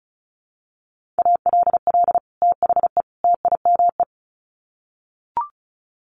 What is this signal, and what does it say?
Morse code, heard as a keyed tone, about 34 wpm, 700 Hz. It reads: ALL THE TIME
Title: Sets of 3 Words 35wpm